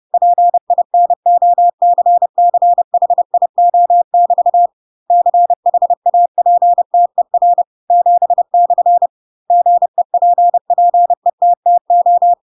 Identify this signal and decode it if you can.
Morse code, heard as a keyed tone, about 30 wpm, 700 Hz. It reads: PINOCCHIO<BT> CHAPTER 7/ GEPPETTO
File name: Pinocchio / Pinocchio ch7 30 30 ik1hge